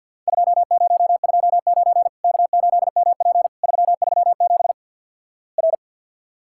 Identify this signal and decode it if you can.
Morse code, heard as a keyed tone, about 48 wpm, 700 Hz. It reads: 2029 K8MP 337 R
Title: Worked All Europe DX Contest - QTC Exchange - 50wpm